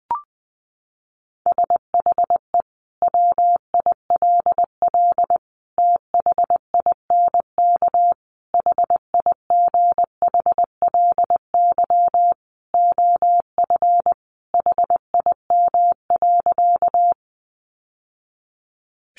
Morse code, heard as a keyed tone, about 20 wpm, 700 Hz. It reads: SHE WILL THINK HIGHLY OF HIM.
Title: Sentences from Top 1100 Words 20wpm